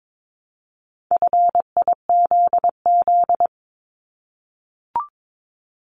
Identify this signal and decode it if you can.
Morse code, heard as a keyed tone, about 22 wpm, 700 Hz. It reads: FIZZ
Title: Multi-letter Phonemes Repeated 3x 22wpm